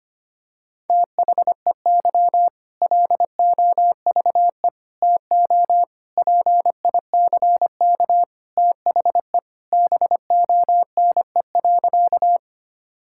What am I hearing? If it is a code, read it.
Morse code, heard as a keyed tone, about 25 wpm, 700 Hz. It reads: THEY LOVE TO PICK THE BONE.
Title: Sentences from Top 1600 Words 25wpm